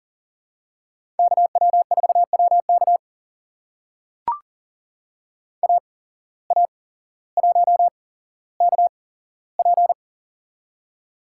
Morse code, heard as a keyed tone, about 39 wpm, spaced as for 17 wpm, 700 Hz. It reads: KW4WK AA1KP
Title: USA Call Signs - 5 Characters - 8x Character Spacing - 40wpm